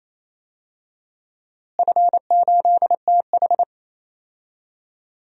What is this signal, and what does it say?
Morse code, heard as a keyed tone, about 28 wpm, 700 Hz. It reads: F8TH
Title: DXCC-100 Call Signs - 4 Characters - Repeated 2x - 28wpm